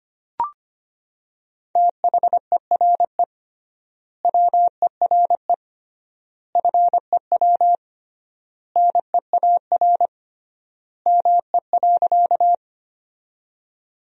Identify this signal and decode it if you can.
Morse code, heard as a keyed tone, about 25 wpm, 700 Hz. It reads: THERE WERE FEW NEAR ME.
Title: 3x Word-Spacing - Sentences from Top 300 Words - 25wpm